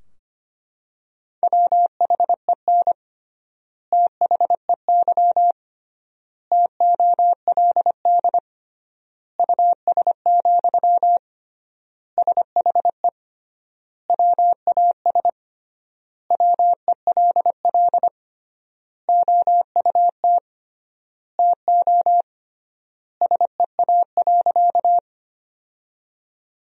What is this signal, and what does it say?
Morse code, heard as a keyed tone, about 25 wpm, 700 Hz. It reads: WHEN THEY TOLD US, SHE WAS WELL OUT TO SEA.
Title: 3x Word-Spacing - Sentences from Top 400 Words - 25wpm